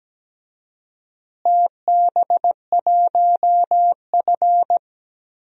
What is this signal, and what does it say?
Morse code, heard as a keyed tone, about 17 wpm, 700 Hz. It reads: TB1F
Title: DXCC-100 Call Signs - 4 Characters - Repeated 3x - 17wpm